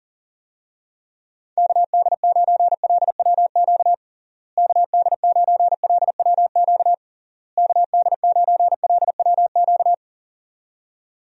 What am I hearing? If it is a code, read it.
Morse code, heard as a keyed tone, about 37 wpm, 700 Hz. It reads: KD9LWQ KD9LWQ KD9LWQ
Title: USA Call Signs - 6 Characters - Repeated 3x - 40wpm